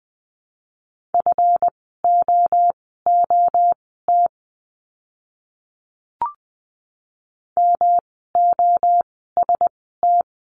Morse code, heard as a keyed tone, about 20 wpm, spaced as for 14 wpm, 700 Hz. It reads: FOOT MOST
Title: 139 Lesson 23f - Letter f - Words 20wpm